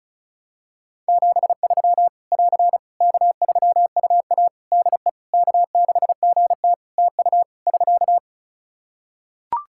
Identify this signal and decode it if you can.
Morse code, heard as a keyed tone, about 36 wpm, 700 Hz. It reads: 73 <AR> K3UA DE K6GT TU <SK>
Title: Practice QSO - Exchange 4 - Ending - 35wpm